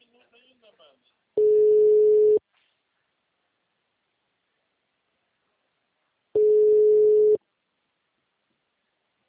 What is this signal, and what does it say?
Ringback tone (Europe, 425 Hz), with speech.